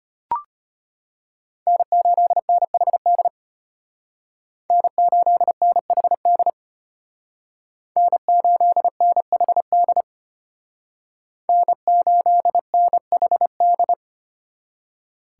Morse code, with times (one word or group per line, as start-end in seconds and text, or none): 1.7-3.3 N8NHD
4.7-6.5 N8NHD
8.0-10.0 N8NHD
11.5-13.9 N8NHD